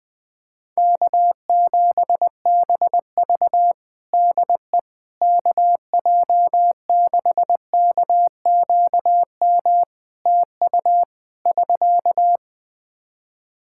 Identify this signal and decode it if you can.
Morse code, heard as a keyed tone, about 20 wpm, 700 Hz. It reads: K7BV DE KJ6KQM TU <SK>
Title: Practice QSO - Exchange 4 - Ending - 20wpm